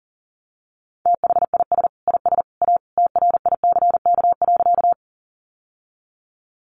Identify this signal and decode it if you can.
Morse code, heard as a keyed tone, about 40 wpm, 700 Hz. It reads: THIS IS A TRICK.
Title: Sentences from Top 1500 Words 40wpm